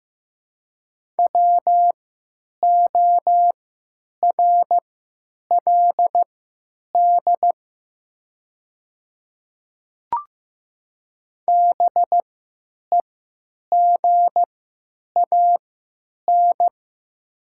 Morse code, heard as a keyed tone, about 15 wpm, spaced as for 9 wpm, 700 Hz. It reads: WORLD BEGAN
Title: 3x Character Spacing - Top 300 Words - 15wpm